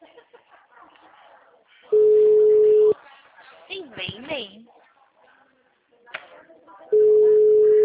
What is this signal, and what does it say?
Ringback tone (Europe, 425 Hz), with speech.